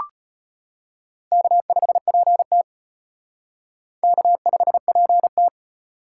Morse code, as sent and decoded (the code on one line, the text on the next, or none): -.- ..... .--. - / -.- ..... .--. -
K5PT K5PT